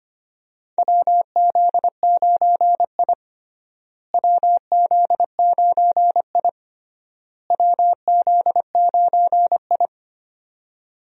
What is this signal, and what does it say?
Morse code, heard as a keyed tone, about 25 wpm, 700 Hz. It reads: WZ9I WZ9I WZ9I
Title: USA Call Signs - 4 Characters - Repeated 3x - 25wpm